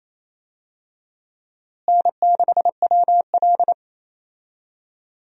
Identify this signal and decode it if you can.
Morse code, heard as a keyed tone, about 28 wpm, 700 Hz. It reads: N6WL
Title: USA Call Signs - 4 Characters - Repeated 3x - 28wpm